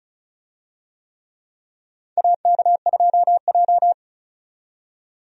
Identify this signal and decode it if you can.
Morse code, heard as a keyed tone, about 36 wpm, 700 Hz. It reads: AK2J
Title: USA Call Signs - 4 Characters - 35wpm